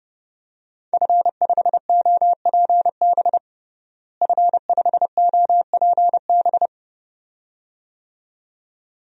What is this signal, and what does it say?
Morse code, heard as a keyed tone, about 32 wpm, 700 Hz. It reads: F5OPB F5OPB
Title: DXCC-100 Call Signs - 5 Characters - Repeated 2x - 30wpm